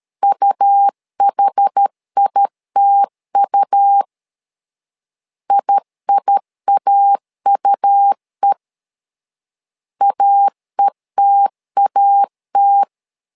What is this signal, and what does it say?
Morse code, heard as a keyed tone, about 13 wpm, 800 Hz. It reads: UHITU IIAUE AETAT
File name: LEZIONI / 2 UHDN / 2 UH 12wpm 1439